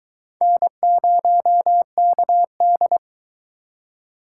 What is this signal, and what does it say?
Morse code, heard as a keyed tone, about 23 wpm, 700 Hz. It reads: N0KD